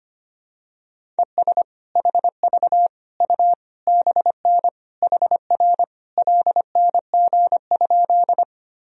Morse code, heard as a keyed tone, about 25 wpm, 700 Hz. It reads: ES HV U BN HR LNG?
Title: Sentences from Top 200 Words and Abbreviations 25wpm